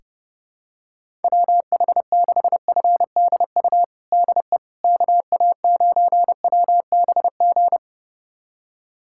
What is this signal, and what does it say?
Morse code, heard as a keyed tone, about 30 wpm, 700 Hz. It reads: WH6FDU DE KA9WBG
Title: USA Callsign DE Callsign - 6 Characters - 30wpm